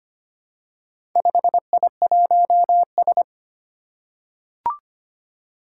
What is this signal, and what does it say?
Morse code, heard as a keyed tone, about 25 wpm, 700 Hz. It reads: 5I1S
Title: Four Challenging Letter-Number Combinations 25wpm